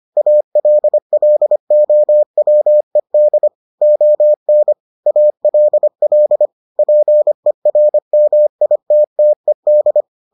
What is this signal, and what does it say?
Morse code, heard as a keyed tone, about 25 wpm, 600 Hz. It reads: ALLOWED ON ALL PERMITTED